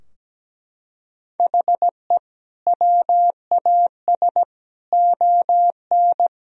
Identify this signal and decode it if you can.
Morse code, heard as a keyed tone, about 17 wpm, 700 Hz. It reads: HE WAS ON
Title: Sentences from Top 1000 Words - Review Part 3 - 17wpm